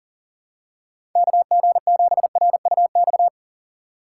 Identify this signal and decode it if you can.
Morse code, heard as a keyed tone, about 40 wpm, 700 Hz. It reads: KG7RUX
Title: USA Call Signs - 2x Character Spacing - 40wpm